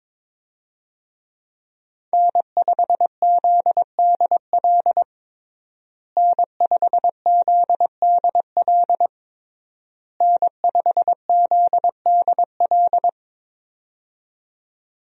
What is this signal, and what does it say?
Morse code, heard as a keyed tone, about 22 wpm, 700 Hz. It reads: N5ZDL N5ZDL N5ZDL
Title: USA Call Signs - 5 Characters - Repeated 3x - 22wpm